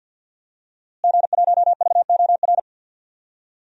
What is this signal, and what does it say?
Morse code, heard as a keyed tone, about 47 wpm, 700 Hz. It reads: G1VOR
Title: DXCC-100 Call Signs - 5 Characters - Repeated 2x - 50wpm